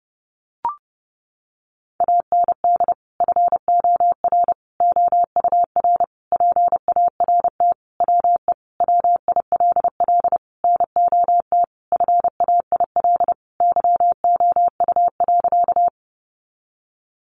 Morse code, heard as a keyed tone, about 30 wpm, 700 Hz. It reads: AND FOR OUR PART WE WILL NOT FAIL YOU.
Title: Sentences from Top 1100 Words 30wpm